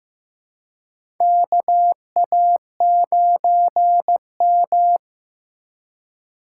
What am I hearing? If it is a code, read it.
Morse code, heard as a keyed tone, about 15 wpm, 700 Hz. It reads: KA9M
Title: USA Call Signs - 4 Characters - 4x Character Spacing - 15wpm